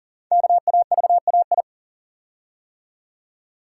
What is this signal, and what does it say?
Morse code, heard as a keyed tone, about 40 wpm, 700 Hz. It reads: KAVAI
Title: Code Groups - 5 Letters - 40wpm